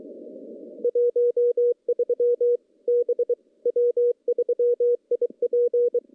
Morse code, heard as a keyed tone, about 23 wpm, 480 Hz. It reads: 13 B W3IP
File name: contests / CW-SS-k5tr- -0200